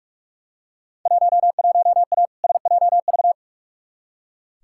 Morse code, heard as a keyed tone, about 43 wpm, 700 Hz. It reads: 11A SJV
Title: Field Day Exchanges Encore - 45wpm